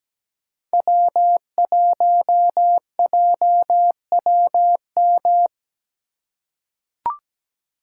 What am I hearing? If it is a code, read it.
Morse code, heard as a keyed tone, about 17 wpm, 700 Hz. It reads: W1JWM
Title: USA Call Signs - 5 Characters - 2x Character Spacing - 17wpm